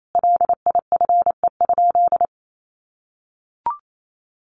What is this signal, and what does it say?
Morse code, heard as a keyed tone, about 28 wpm, 700 Hz. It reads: LIFE?
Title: Sentences from Top 1400 Words 28wpm